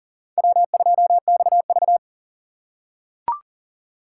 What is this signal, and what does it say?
Morse code, heard as a keyed tone, about 40 wpm, 700 Hz. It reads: W2XV
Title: USA Call Signs - 4 Characters - 6x Character Spacing - 40wpm